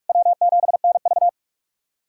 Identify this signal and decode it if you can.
Morse code, heard as a keyed tone, about 46 wpm, 700 Hz. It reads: W7NV
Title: USA Callsign DE Callsign 45wpm